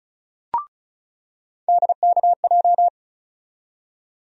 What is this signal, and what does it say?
Morse code, heard as a keyed tone, about 36 wpm, 700 Hz. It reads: DKJ
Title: USA Call Sign Postfixes - 3 Letters 35wpm